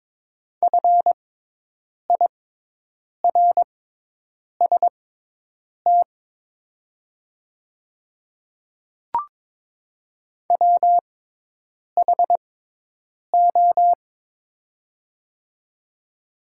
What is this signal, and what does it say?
Morse code, heard as a keyed tone, about 22 wpm, spaced as for 8 wpm, 700 Hz. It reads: FIRST WHO
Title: 6x Character Spacing - Top 100 Words - 22wpm